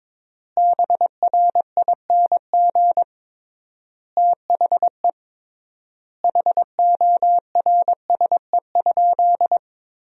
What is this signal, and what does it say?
Morse code, heard as a keyed tone, about 22 wpm, 700 Hz. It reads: BRING THE HORSE?
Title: 3x Word-Spacing - Sentences from Top 500 Words - 22wpm